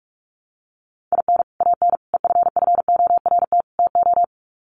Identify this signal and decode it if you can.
Morse code, heard as a keyed tone, about 44 wpm, 700 Hz. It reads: IN AN EFFORT TO
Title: Sets of 4 Words 45wpm